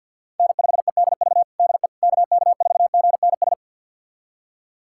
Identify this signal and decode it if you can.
Morse code, heard as a keyed tone, about 51 wpm, 700 Hz. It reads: N5EDV DE KK4GNS
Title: USA Callsign DE Callsign 50wpm